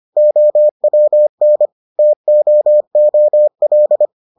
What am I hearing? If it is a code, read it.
Morse code, heard as a keyed tone, about 25 wpm, 600 Hz. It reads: OWN TOOL